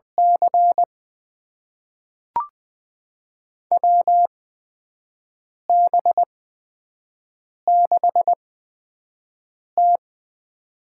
Morse code, heard as a keyed tone, about 20 wpm, spaced as for 5 wpm, 700 Hz. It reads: C WB6T
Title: USA Call Signs - 4 Characters - 8x Character Spacing - 20wpm